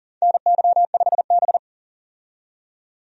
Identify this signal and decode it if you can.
Morse code, heard as a keyed tone, about 40 wpm, 700 Hz. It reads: NY5B
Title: USA Call Signs - 4 Characters - 40wpm